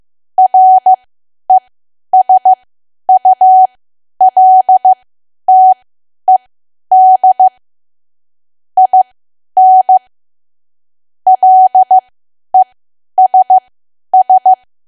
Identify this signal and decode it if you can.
Morse code, heard as a keyed tone, about 15 wpm, spaced as for 10 wpm, 750 Hz. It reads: RESULTED IN LESS
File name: cwa / W1AW-Level2 / 10WPM